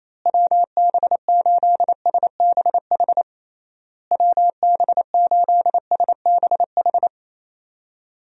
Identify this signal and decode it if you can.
Morse code, heard as a keyed tone, about 27 wpm, 700 Hz. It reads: WB8SBH WB8SBH
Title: USA Call Signs - 6 Characters - Repeated 3x - 28wpm